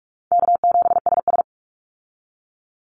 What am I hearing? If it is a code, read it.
Morse code, heard as a keyed tone, about 44 wpm, 700 Hz. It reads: K7SS
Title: Easy Call Signs Repeated 2x - 45wpm